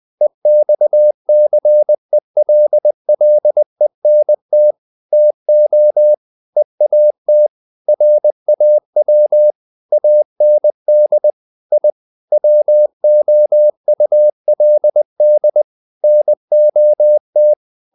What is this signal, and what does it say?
Morse code, heard as a keyed tone, about 20 wpm, 600 Hz. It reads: EXCELLENT TO EAT RAW AND I WOULD NOT
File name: the old man and the sea / the old man and the sea part 1 20wpm